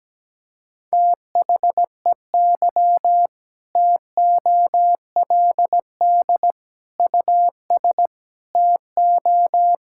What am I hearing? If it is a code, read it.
Morse code, heard as a keyed tone, about 17 wpm, 700 Hz. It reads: THEY TOLD US TO
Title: Sentences from Top 1600 Words 17wpm